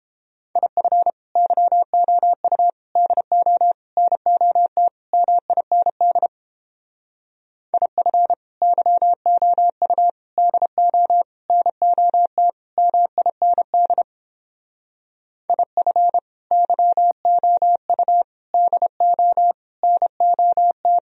Morse code, heard as a keyed tone, about 31 wpm, 700 Hz. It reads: IF YOU DO NOT MIND IF YOU DO NOT MIND IF YOU DO NOT